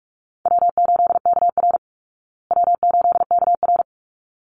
Morse code, heard as a keyed tone, about 44 wpm, 700 Hz. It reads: W8KR W8KR
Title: Easy Call Signs Repeated 2x - 45wpm